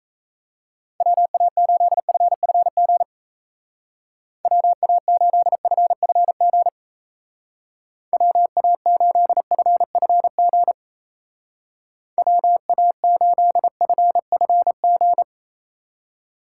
Morse code, heard as a keyed tone, about 35 wpm, 700 Hz. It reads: WA8FFG WA8FFG WA8FFG WA8FFG
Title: Speed Racing - USA Call Signs 28wpm